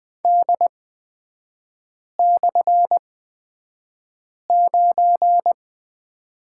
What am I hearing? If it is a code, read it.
Morse code, heard as a keyed tone, about 20 wpm, spaced as for 5 wpm, 700 Hz. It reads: D/9
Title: 250 Lesson 40e - BK - Three Characters 20wpm